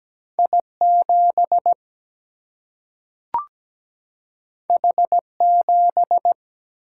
Morse code, heard as a keyed tone, about 17 wpm, 700 Hz. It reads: I 7 H 7
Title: Single-Letter Single-Number 17wpm